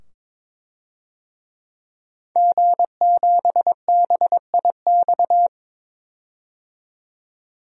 Morse code, heard as a keyed tone, about 22 wpm, 700 Hz. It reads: G7BIX
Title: DXCC-100 Call Signs - 5 Characters - 22wpm